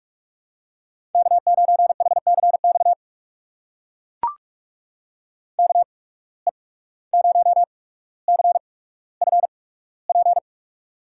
Morse code, heard as a keyed tone, about 44 wpm, spaced as for 18 wpm, 700 Hz. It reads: K9HCX KE0CFP
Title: USA Call Signs - 8x Character Spacing - 45wpm